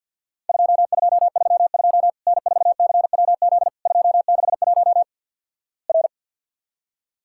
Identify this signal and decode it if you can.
Morse code, heard as a keyed tone, about 52 wpm, 700 Hz. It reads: 2232 N4CWZ 261 R
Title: Worked All Europe DX Contest - QTC Exchange - 50wpm